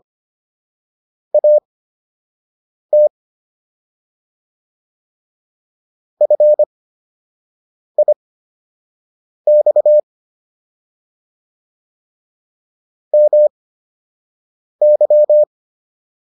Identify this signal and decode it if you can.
Morse code, heard as a keyed tone, about 25 wpm, spaced as for 6 wpm, 600 Hz. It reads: AT FIX MY